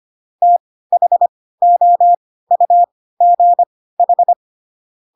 Morse code, heard as a keyed tone, about 25 wpm, spaced as for 16 wpm, 700 Hz. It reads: THOUGH
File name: Pinocchio / Pinocchio ch5 10 25 ik1hge